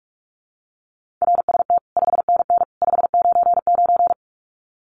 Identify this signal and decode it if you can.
Morse code, heard as a keyed tone, about 44 wpm, 700 Hz. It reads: RST 5NN 599
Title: Common QSO Phrases 45wpm